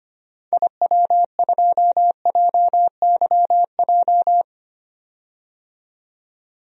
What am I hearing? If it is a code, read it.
Morse code, heard as a keyed tone, about 25 wpm, 690 Hz. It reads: IW2JYJ
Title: DXCC-100 Call Signs - 6 Characters - 25wpm